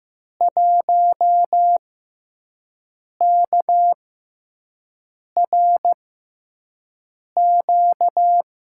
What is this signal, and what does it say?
Morse code, heard as a keyed tone, about 15 wpm, spaced as for 5 wpm, 700 Hz. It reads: 1KRQ
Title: USA Call Signs - 5 Characters - 6x Character Spacing - 15wpm